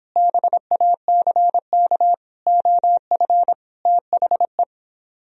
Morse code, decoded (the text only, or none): BACK OF THE